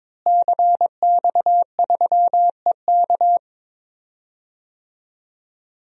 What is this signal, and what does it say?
Morse code, heard as a keyed tone, about 22 wpm, 700 Hz. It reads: CX3EK
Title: DXCC-100 Call Signs - 5 Characters - Repeated 2x - 22wpm